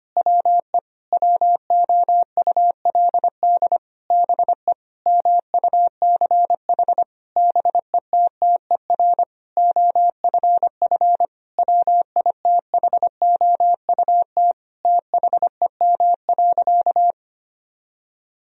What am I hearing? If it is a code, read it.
Morse code, heard as a keyed tone, about 25 wpm, 700 Hz. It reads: WE WOULD BE MUCH BETTER OFF WITHOUT THEM.
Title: Sentences from Top 1000 Words - Review Part 3 - 25wpm